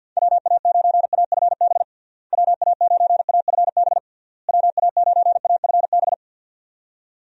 Morse code, heard as a keyed tone, about 52 wpm, 700 Hz. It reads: WA9AFB WA9AFB WA9AFB
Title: USA Call Signs - 6 Characters - Repeated 3x - 50wpm